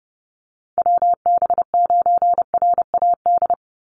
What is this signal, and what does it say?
Morse code, heard as a keyed tone, about 30 wpm, 700 Hz. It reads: WB9RAD
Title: USA Call Signs 30wpm